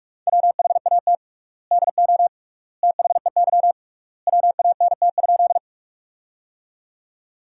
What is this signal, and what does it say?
Morse code, heard as a keyed tone, about 42 wpm, 700 Hz. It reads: WHAT DO THEY WANT?
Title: 3x Word-Spacing - Sentences from Top 200 Words - 45wpm